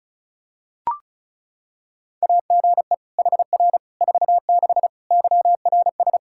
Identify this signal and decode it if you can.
Morse code, heard as a keyed tone, about 34 wpm, 700 Hz. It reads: AGE HR 46 YRS
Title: K1USN SST QSO Practice 35wpm